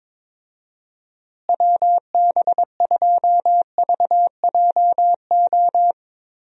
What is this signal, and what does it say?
Morse code, heard as a keyed tone, about 22 wpm, 700 Hz. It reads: WB2VJO
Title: USA Call Signs - 6 Characters - Repeated 2x - 22wpm